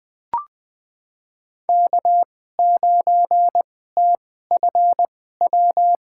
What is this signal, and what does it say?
Morse code, heard as a keyed tone, about 20 wpm, spaced as for 14 wpm, 700 Hz. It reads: K9TFW
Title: USA Call Signs - 5 Characters - 2x Character Spacing - 20wpm